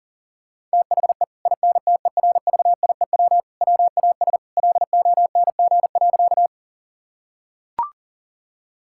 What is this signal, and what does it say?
Morse code, heard as a keyed tone, about 40 wpm, 700 Hz. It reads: THE INTERVIEW WAS LONG.
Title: Sentences from Top 1600 Words 40wpm